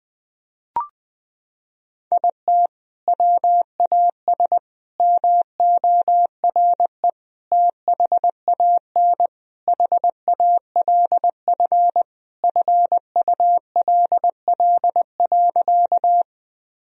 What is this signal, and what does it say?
Morse code, heard as a keyed tone, about 21 wpm, 700 Hz. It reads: IT WAS MORE THAN HALF FULL.
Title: Sentences from Top 1000 Words - Review Part 3 - 20wpm